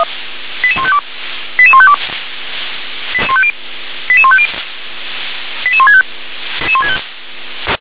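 Five-tone selective call (ZVEI), heard as quick runs of five tones, tones of about 70 ms. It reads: 88152 88152 88150 88165
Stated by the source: ZVEI five-tone call received by SDR.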